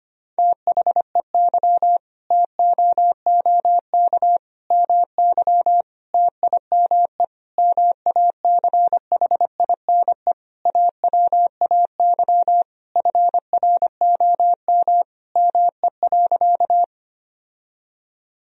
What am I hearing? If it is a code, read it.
Morse code, heard as a keyed tone, about 25 wpm, 700 Hz. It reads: THEY TOOK MY TIME MACHINE AWAY FROM ME.
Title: Sentences from Top 1000 Words - Review Part 2 - 25wpm